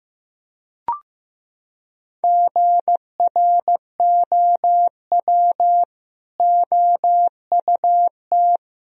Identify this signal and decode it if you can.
Morse code, heard as a keyed tone, about 16 wpm, 700 Hz. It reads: GROW OUT
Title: Sentences from Top 1000 Words - Review Part 1 - 15wpm